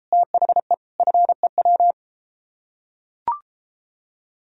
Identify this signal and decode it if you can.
Morse code, heard as a keyed tone, about 34 wpm, 700 Hz. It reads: THE FEW